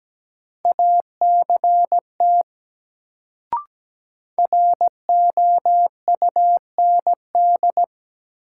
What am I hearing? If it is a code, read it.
Morse code, heard as a keyed tone, about 17 wpm, 700 Hz. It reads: ACT ROUND
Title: Sentences from Top 200 Words - New Word Drill - 17wpm